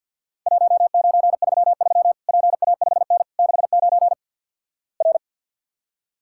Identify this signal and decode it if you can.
Morse code, heard as a keyed tone, about 52 wpm, 700 Hz. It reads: 1933 PA5N 69 R
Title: Worked All Europe DX Contest - QTC Exchange - 50wpm